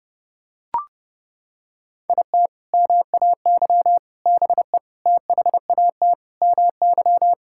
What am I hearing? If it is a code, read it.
Morse code, heard as a keyed tone, about 30 wpm, 700 Hz. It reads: IT MAY BE THAT MY
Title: Sentences from Top 1600 Words 30wpm